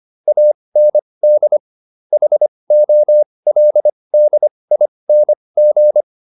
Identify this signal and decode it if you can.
Morse code, heard as a keyed tone, about 25 wpm, spaced as for 20 wpm, 600 Hz. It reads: AND HOLDING